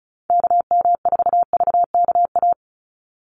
Morse code, heard as a keyed tone, about 34 wpm, 700 Hz. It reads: KM4VKA
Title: USA Call Signs 35wpm